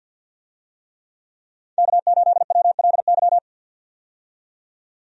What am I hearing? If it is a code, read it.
Morse code, heard as a keyed tone, about 51 wpm, 700 Hz. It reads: K8WLY
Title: USA Call Signs - 5 Characters - 50wpm